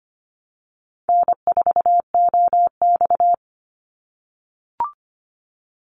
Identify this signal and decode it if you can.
Morse code, heard as a keyed tone, about 25 wpm, 700 Hz. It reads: N4OX
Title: Easy Call Signs - 25wpm